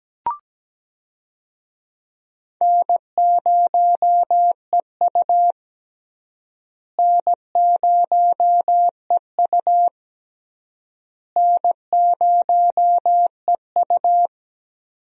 Morse code, heard as a keyed tone, about 17 wpm, 700 Hz. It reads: N0EU N0EU N0EU
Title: USA Call Signs - 4 Characters - Repeated 3x - 17wpm